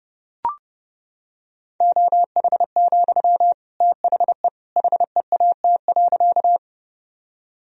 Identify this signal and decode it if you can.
Morse code, heard as a keyed tone, about 32 wpm, 700 Hz. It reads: OH, THE HEAT.
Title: Sentences from Top 1000 Words - Review Part 3 - 30wpm